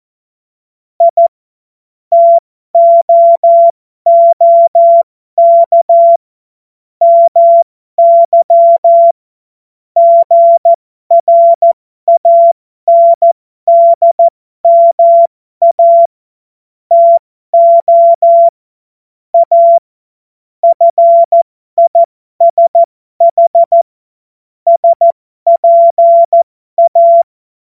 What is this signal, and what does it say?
Morse code, heard as a keyed tone, about 14 wpm, spaced as for 12 wpm, 680 Hz. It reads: I TOOK MY GRANDMA TO A FISH SPA
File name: Morsekurs / Witze